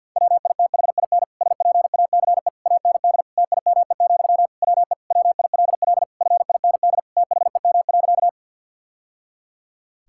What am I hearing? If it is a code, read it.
Morse code, heard as a keyed tone, about 50 wpm, 700 Hz. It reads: WITHIN SPACE AND TIME, WE WILL FIND THEM.
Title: Sentences from Top 1000 Words - Review Part 5 - 50wpm